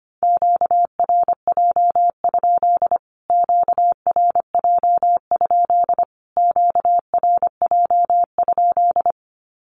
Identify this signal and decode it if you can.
Morse code, heard as a keyed tone, about 25 wpm, 700 Hz. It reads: QRJ? QRJ? QRJ?
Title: Q-Codes Repeated 3x 25wpm